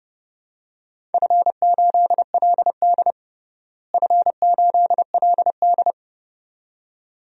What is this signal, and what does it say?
Morse code, heard as a keyed tone, about 32 wpm, 700 Hz. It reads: F8LD F8LD
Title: DXCC-100 Call Signs - 4 Characters - Repeated 2x - 30wpm